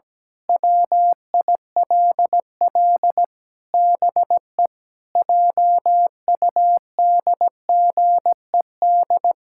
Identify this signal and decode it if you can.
Morse code, heard as a keyed tone, about 17 wpm, 700 Hz. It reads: WILL BE JUDGED
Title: Pangram Challenge v2 - 17wpm